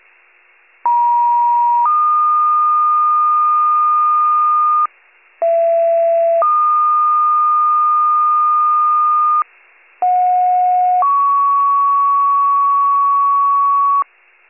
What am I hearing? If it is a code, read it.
Two-tone page, heard as a one-second tone, then a three-second tone, about 950 then 1220 Hz, about 670 then 1220 Hz, about 730 then 1120 Hz. It is heard from 0.8 to 14.0 s.